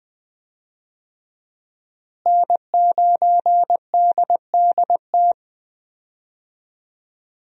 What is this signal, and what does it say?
Morse code, heard as a keyed tone, about 20 wpm, 700 Hz. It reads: N9DDT
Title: USA Call Signs - 5 Characters - Repeated 3x - 20wpm